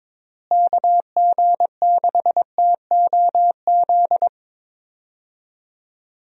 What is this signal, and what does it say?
Morse code, heard as a keyed tone, about 22 wpm, 700 Hz. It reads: KG6TOZ
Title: USA Call Signs - 6 Characters - Repeated 2x - 22wpm